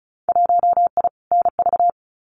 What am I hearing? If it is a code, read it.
Morse code, heard as a keyed tone, about 36 wpm, 700 Hz. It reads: 1I NV
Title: Winter Field Day Exchanges Encore 35wpm